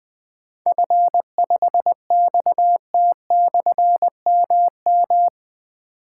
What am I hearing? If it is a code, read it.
Morse code, heard as a keyed tone, about 20 wpm, 700 Hz. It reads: F5XT/MM
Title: Difficult Call Signs 20wpm